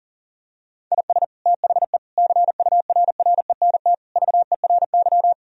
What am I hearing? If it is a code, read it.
Morse code, heard as a keyed tone, about 40 wpm, 700 Hz. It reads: IS THE CURRENT VERY
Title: Sentences from Top 1000 Words - Review Part 4 - 40wpm